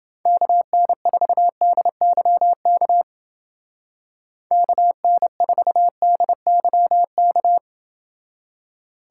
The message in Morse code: KN4DYK KN4DYK

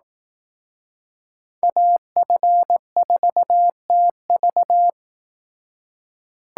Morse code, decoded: AF4TV

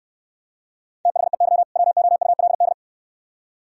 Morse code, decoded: THEY WOULD

